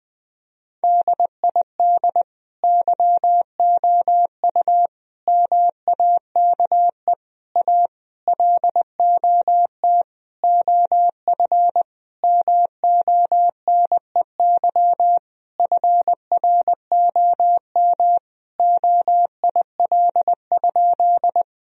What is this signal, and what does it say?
Morse code, heard as a keyed tone, about 21 wpm, 700 Hz. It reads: DID YOU MAKE A LOT OF MONEY FROM OIL?
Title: Sentences from Top 1000 Words - Review Part 4 - 20wpm